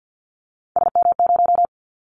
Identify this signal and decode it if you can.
Morse code, heard as a keyed tone, about 50 wpm, 700 Hz. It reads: SM0
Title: Callsign Prefixes for Top 25 DXCC - 50wpm